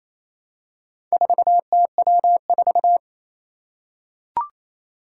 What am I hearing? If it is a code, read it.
Morse code, heard as a keyed tone, about 28 wpm, 700 Hz. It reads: 4TW4
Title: Four Challenging Letter-Number Combinations 28wpm